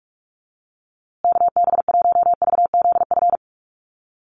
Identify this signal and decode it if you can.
Morse code, heard as a keyed tone, about 44 wpm, 700 Hz. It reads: KB1VZF
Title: USA Call Signs Repeated 2x 45wpm